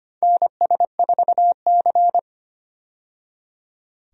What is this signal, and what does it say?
Morse code, heard as a keyed tone, about 25 wpm, 700 Hz. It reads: NS4C
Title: USA Call Signs - 4 Characters - 25wpm